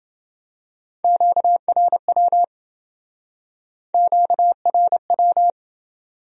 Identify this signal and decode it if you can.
Morse code, heard as a keyed tone, about 31 wpm, 700 Hz. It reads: QRW QRW